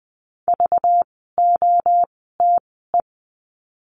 Morse code, heard as a keyed tone, about 20 wpm, spaced as for 14 wpm, 700 Hz. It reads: VOTE
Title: 192 Lesson 31f - Letter v - Words 20wpm